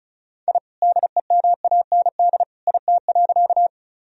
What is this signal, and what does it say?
Morse code, heard as a keyed tone, about 34 wpm, 700 Hz. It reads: I DEMAND IT.
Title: Sentences from Top 1000 Words - Review Part 4 - 35wpm